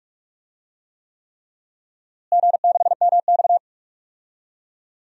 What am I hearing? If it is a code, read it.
Morse code, heard as a keyed tone, about 41 wpm, 700 Hz. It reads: G6MX
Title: DXCC-100 Call Signs - 4 Characters - Repeated 3x - 45wpm